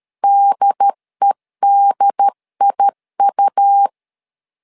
Morse code, heard as a keyed tone, about 13 wpm, 800 Hz. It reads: DEDIU
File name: LEZIONI / 2 UHDN / 2 UHD 12wpm 1540